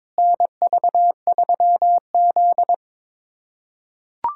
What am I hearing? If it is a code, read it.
Morse code, heard as a keyed tone, about 22 wpm, 700 Hz. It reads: NV3Z
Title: USA Callsign DE Callsign - 4 Characters - 22wpm